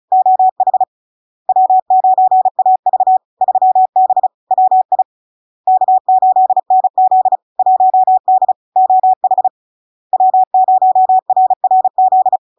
Morse code, heard as a keyed tone, about 36 wpm, 750 Hz. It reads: OH W9AV 3B WI K8NZ 1D OH W0RRZ
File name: LARK-FD-2022-Log-in-CW-35wpm-750-Hz